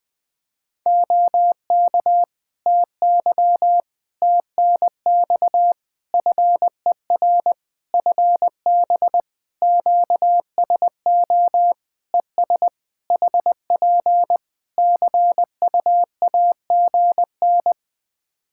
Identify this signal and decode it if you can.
Morse code, heard as a keyed tone, about 21 wpm, 700 Hz. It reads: OK TY TNX FER FB QSO ES HP CUAGN
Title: K1USN SST QSO Practice 20wpm